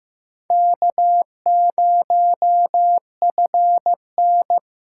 Morse code, heard as a keyed tone, about 15 wpm, 700 Hz. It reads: K0FN
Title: USA Call Signs - 2x Character Spacing - 15wpm